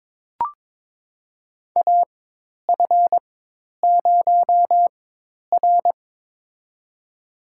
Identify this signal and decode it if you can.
Morse code, heard as a keyed tone, about 22 wpm, spaced as for 10 wpm, 700 Hz. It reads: AF0R
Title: USA Call Signs - 4 Characters - 4x Character Spacing - 22wpm